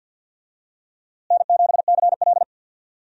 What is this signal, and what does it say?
Morse code, heard as a keyed tone, about 46 wpm, 700 Hz. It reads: N7CL
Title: USA Call Signs - 4 Characters - Repeated 3x - 50wpm